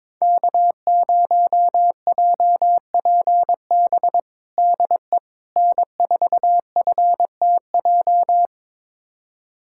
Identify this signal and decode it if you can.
Morse code, heard as a keyed tone, about 21 wpm, 700 Hz. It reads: K0JPB DE N4FTJ
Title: USA Callsign DE Callsign - 5 Characters - 22wpm